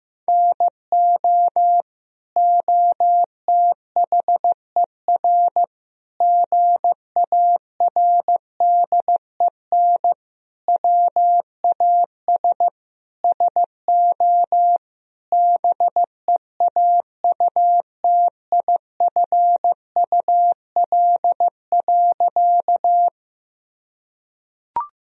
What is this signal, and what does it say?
Morse code, heard as a keyed tone, about 15 wpm, 700 Hz. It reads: NO OTHER GARDEN WAS SO BEAUTIFUL.
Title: Sentences from Top 1000 Words - Review Part 4 - 15wpm